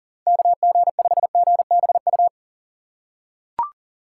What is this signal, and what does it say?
Morse code, heard as a keyed tone, about 40 wpm, 700 Hz. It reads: KG5GBU
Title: USA Call Signs - 6 Characters - 4x Character Spacing - 40wpm